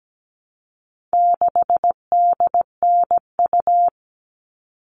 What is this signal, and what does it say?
Morse code, heard as a keyed tone, about 17 wpm, 700 Hz. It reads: 6DNU
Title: All Cut Numbers - 4-Digit 17wpm